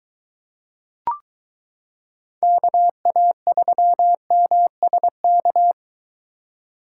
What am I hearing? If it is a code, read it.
Morse code, heard as a keyed tone, about 22 wpm, 700 Hz. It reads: KA3MSK